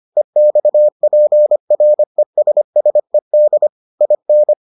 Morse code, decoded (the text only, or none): EXPRESSED IN